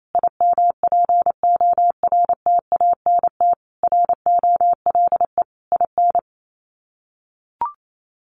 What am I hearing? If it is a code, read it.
Morse code, heard as a keyed tone, about 28 wpm, 700 Hz. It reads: IMPORTANT ROLE IN
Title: Sets of 5 Words Encore2 28wpm